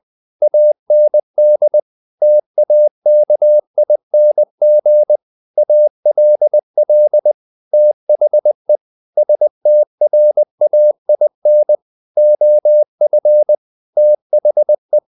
Morse code, heard as a keyed tone, about 20 wpm, 600 Hz. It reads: AND TAKING ALL THE STRAIN OF THE
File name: the old man and the sea / the old man and the sea part 1 20wpm